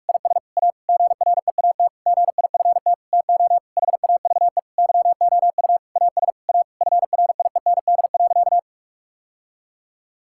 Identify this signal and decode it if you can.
Morse code, heard as a keyed tone, about 41 wpm, 700 Hz. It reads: IS A GREAT GIFT TO HAVE YOU AS A FRIEND.
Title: Sentences from Top 1000 Words - Review Part 6 - 45wpm